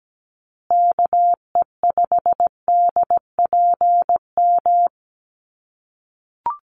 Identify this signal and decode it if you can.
Morse code, heard as a keyed tone, about 17 wpm, 700 Hz. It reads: KE5DPM
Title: USA Call Signs Repeated 2x 17wpm